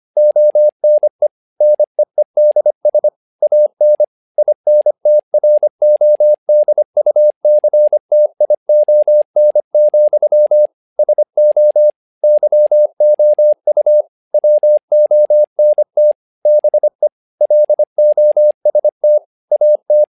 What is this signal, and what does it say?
Morse code, heard as a keyed tone, about 25 wpm, 600 Hz. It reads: ONE NEEDS AN INTRODUCTION, SO YOU WONT BE LOST AT
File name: learning center / tower trouble / tower trouble 25wpm